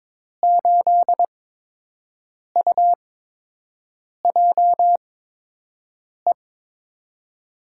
Morse code, heard as a keyed tone, about 22 wpm, spaced as for 6 wpm, 700 Hz. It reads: 8UJE
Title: USA Call Signs - 6 Characters - 8x Character Spacing - 22wpm